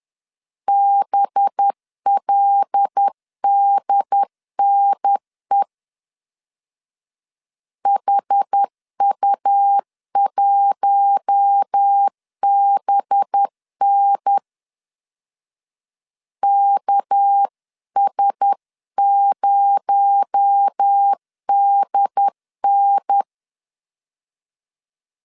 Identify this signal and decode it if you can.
Morse code, heard as a keyed tone, about 10 wpm, 800 Hz. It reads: BLDNE HU1BN KS0DN